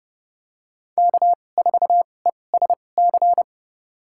Morse code, heard as a keyed tone, about 30 wpm, spaced as for 22 wpm, 700 Hz. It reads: K4ESC
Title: USA Call Signs - 5 Characters - 2x Character Spacing - 30wpm